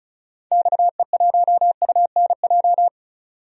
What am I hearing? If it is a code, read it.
Morse code, heard as a keyed tone, about 34 wpm, 700 Hz. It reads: XE1UNJ
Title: DXCC-100 Call Signs - 6 Characters - 35wpm